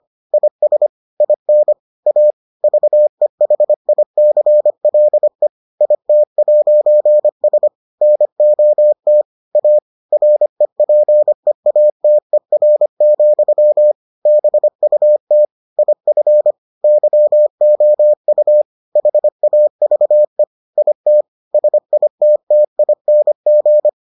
Morse code, heard as a keyed tone, about 25 wpm, 600 Hz. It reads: IS IN A VEHICLE IT'S NOT A REPEATER, BUT IF YOU HAVE IT SITTING